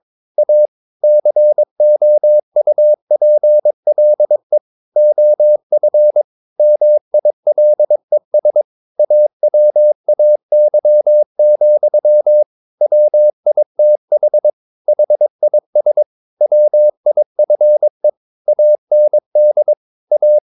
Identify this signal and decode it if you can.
Morse code, heard as a keyed tone, about 22 wpm, 600 Hz. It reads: A COUPLE OF MILES AWAY, WITH HIS WIFE AND A
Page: Romanen Never af Ken Follet på engelsk.